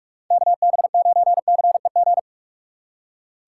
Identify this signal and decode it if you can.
Morse code, heard as a keyed tone, about 42 wpm, 700 Hz. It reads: KB9CEG
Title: USA Call Signs - 6 Characters - 4x Character Spacing - 45wpm